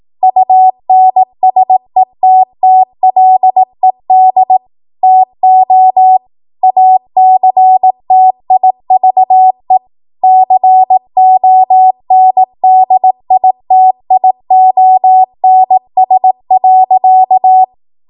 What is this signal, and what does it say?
Morse code, heard as a keyed tone, about 18 wpm, 750 Hz. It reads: UNSETTLED TO ACTIVE CONDITIONS.
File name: Morse / 18WPM